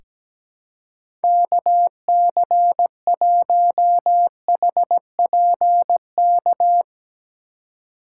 Morse code, heard as a keyed tone, about 17 wpm, 700 Hz. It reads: KC1HPK
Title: USA Call Signs - 6 Characters - 3x Character Spacing - 17wpm